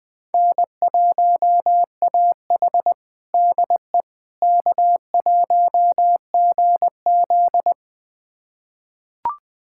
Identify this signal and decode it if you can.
Morse code, heard as a keyed tone, about 20 wpm, 700 Hz. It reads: N1AH DE K1GZ
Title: USA Callsign DE Callsign - 4 Characters - 20wpm